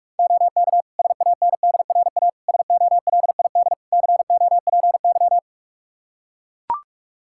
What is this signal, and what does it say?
Morse code, heard as a keyed tone, about 42 wpm, 700 Hz. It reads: OK SANDRA SOLID COPY
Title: Practice QSO - Exchange 3 - More Info - 45wpm